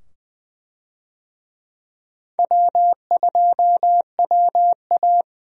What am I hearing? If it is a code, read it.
Morse code, heard as a keyed tone, about 20 wpm, 700 Hz. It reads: W2WA
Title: USA Call Signs - 4 Characters - 20wpm